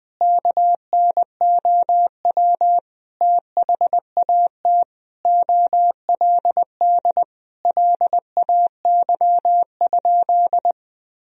Morse code, heard as a keyed tone, about 21 wpm, 700 Hz. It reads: KNOW THAT OLD LAY?
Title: Sentences from Top 400 Words and Abbreviations 20wpm